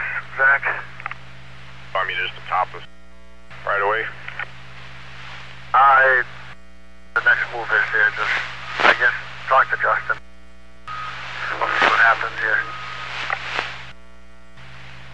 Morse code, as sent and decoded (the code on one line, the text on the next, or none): none